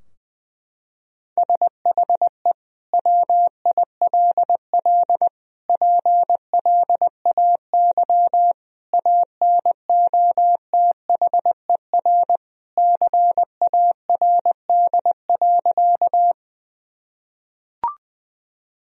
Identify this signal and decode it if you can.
Morse code, heard as a keyed tone, about 21 wpm, 700 Hz. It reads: SHE WILL PLAY ANOTHER CARD.
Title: Sentences from Top 1000 Words - Review Part 4 - 20wpm